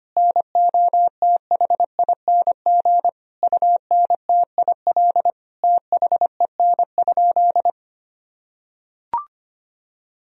Morse code, heard as a keyed tone, about 25 wpm, 700 Hz. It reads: NOTHING UNTIL THEN?
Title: Sentences from Top 1000 Words - Review Part 1 - 25wpm